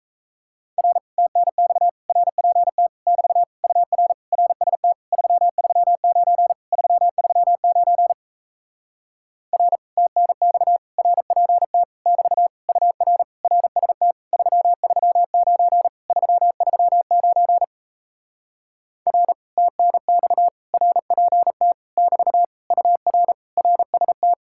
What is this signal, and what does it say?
Morse code, heard as a keyed tone, about 38 wpm, 700 Hz. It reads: R TNX RPT <BT> UR RST 339 339 R TNX RPT <BT> UR RST 339 339 R TNX RPT <BT> UR RST